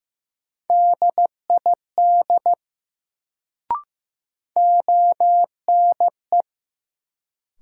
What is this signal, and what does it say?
Morse code, heard as a keyed tone, about 15 wpm, 700 Hz. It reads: DID ONE
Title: Sets of 2 Words - New Word Drill - 15wpm